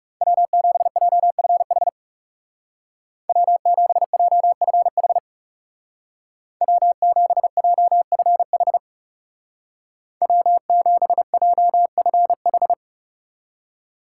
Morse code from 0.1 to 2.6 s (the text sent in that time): W7JFH